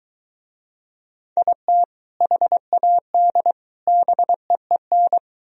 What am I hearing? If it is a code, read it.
Morse code, heard as a keyed tone, about 23 wpm, 700 Hz. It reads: IT HAD BEEN